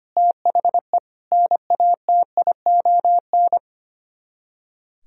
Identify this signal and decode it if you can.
Morse code, heard as a keyed tone, about 33 wpm, 700 Hz. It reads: THE NATION